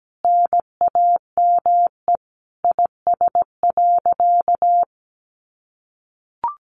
Morse code, heard as a keyed tone, about 17 wpm, 700 Hz. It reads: NAME IS.
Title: Sentences from Top 1100 Words 17wpm